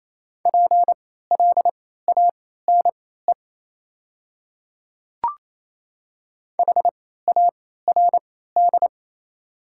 Morse code, heard as a keyed tone, about 28 wpm, spaced as for 16 wpm, 700 Hz. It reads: PLANE HARD
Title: 3x Character Spacing - Top 500 Words - 28wpm